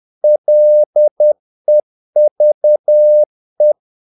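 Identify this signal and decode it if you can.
Morse code, heard as a keyed tone, about 10 wpm, 600 Hz. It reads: LEVE
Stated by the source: Morse practice of short Danish words at 10 wpm, from CW med Gnister.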